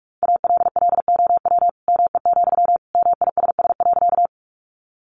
Morse code, heard as a keyed tone, about 44 wpm, 700 Hz. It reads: ALLOW ME, MISS.
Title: Sentences from Top 700 Words 45wpm